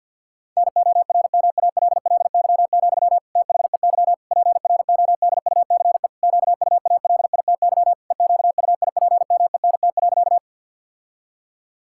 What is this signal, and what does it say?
Morse code, heard as a keyed tone, about 51 wpm, 700 Hz. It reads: NORMALLY, THEY PRODUCE QUALITY EQUIPMENT.
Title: Sentences from Top 1000 Words - Review Part 6 - 50wpm